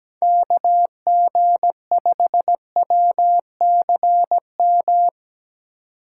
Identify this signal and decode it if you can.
Morse code, heard as a keyed tone, about 17 wpm, 700 Hz. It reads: KG5WCM
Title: USA Call Signs - 3x Character Spacing - 17wpm